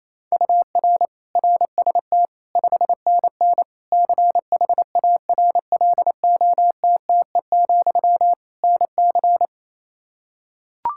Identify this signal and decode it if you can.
Morse code, heard as a keyed tone, about 28 wpm, 700 Hz. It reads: UR RST 5NN CHARLOTTE, NC
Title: K1USN SST QSO Practice 28wpm